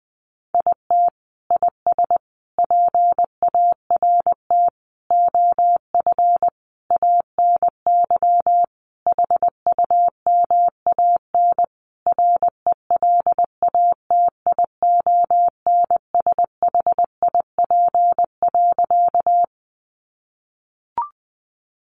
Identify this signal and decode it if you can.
Morse code, heard as a keyed tone, about 20 wpm, 700 Hz. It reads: IT IS PART OF ANY HUMAN RELATIONSHIP.
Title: Sentences from Top 1100 Words 20wpm